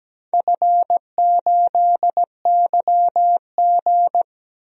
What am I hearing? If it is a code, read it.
Morse code, heard as a keyed tone, about 17 wpm, 700 Hz. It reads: F8YG
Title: DXCC-100 Call Signs - 4 Characters - 17wpm